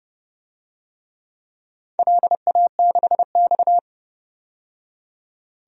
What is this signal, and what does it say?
Morse code, heard as a keyed tone, about 32 wpm, 700 Hz. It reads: LA6X
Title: DXCC-100 Call Signs - 4 Characters - 30wpm